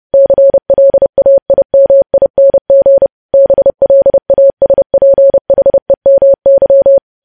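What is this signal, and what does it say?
Morse code, heard as a keyed tone, about 30 wpm, 550 Hz. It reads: CLAIMING BLASPHEMY
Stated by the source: News headlines sent in Morse at 30 words per minute.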